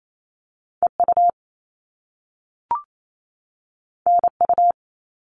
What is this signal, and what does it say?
Morse code, heard as a keyed tone, about 28 wpm, 700 Hz. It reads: EU NU
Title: All Cut Numbers - 2-Digit 28wpm